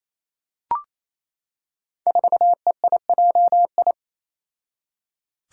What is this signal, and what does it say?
Morse code, heard as a keyed tone, about 28 wpm, 700 Hz. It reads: 4EIJI
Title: Five Challenging Letter-Number Combinations 28wpm